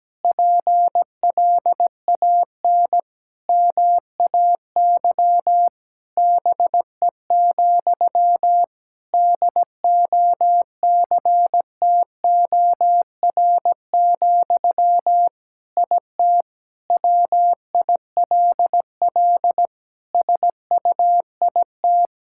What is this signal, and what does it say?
Morse code, heard as a keyed tone, about 17 wpm, 700 Hz. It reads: PLAN MAY BE, DOCTOR, IT WILL SUIT
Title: Sentences from Top 1600 Words 17wpm